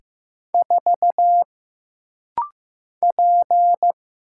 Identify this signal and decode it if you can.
Morse code, heard as a keyed tone, about 15 wpm, 700 Hz. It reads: 4 P
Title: Single Letter-Number Alt - Rapid-Fire 15wpm